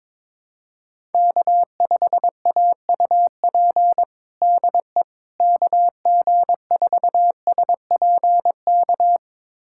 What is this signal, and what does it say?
Morse code, heard as a keyed tone, about 21 wpm, 700 Hz. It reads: K5AUP DE KG4SPK
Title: USA Callsign DE Callsign 22wpm